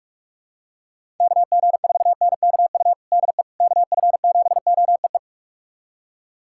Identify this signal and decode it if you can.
Morse code, heard as a keyed tone, about 45 wpm, 700 Hz. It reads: KG4NKU DE KF7OEE
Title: USA Callsign DE Callsign - 6 Characters - 45wpm